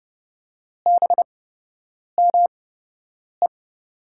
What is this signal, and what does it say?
Morse code, heard as a keyed tone, about 30 wpm, spaced as for 8 wpm, 700 Hz. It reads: BME
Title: USA Call Signs - 8x Character Spacing - 30wpm